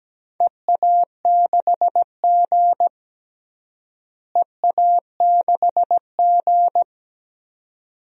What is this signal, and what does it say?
Morse code, heard as a keyed tone, about 17 wpm, 700 Hz. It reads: EA6G EA6G
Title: DXCC-100 Call Signs - 4 Characters - Repeated 2x - 17wpm